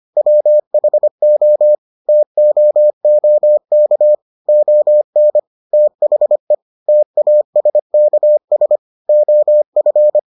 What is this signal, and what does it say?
Morse code, heard as a keyed tone, about 25 wpm, 600 Hz. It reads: WHO TOOK ON THE TASKS OF